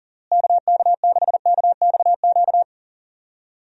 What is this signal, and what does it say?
Morse code, heard as a keyed tone, about 40 wpm, 700 Hz. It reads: KK6KXQ
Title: USA Call Signs - 6 Characters - Repeated 2x - 40wpm